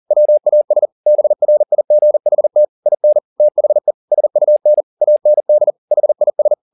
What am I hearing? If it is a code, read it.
Morse code, heard as a keyed tone, about 40 wpm, 600 Hz. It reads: WAS BRIGHT IN THE SUN AND HIS